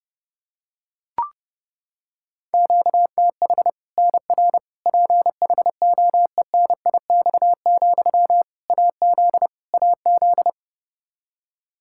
Morse code, heard as a keyed tone, about 29 wpm, 700 Hz. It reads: QTH NR PHOENIX, AZ AZ
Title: Practice QSO - Exchange 1 - Beginning - 30wpm